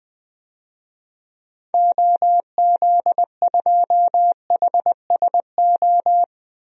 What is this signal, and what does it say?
Morse code, heard as a keyed tone, about 20 wpm, 700 Hz. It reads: OZ2HSO
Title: DXCC-100 Call Signs - 6 Characters - 20wpm